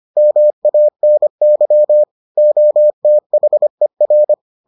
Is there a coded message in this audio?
Yes: Morse code from 0.2 to 4.3 s, heard as a keyed tone, about 25 wpm, 600 Hz. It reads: MANY OTHER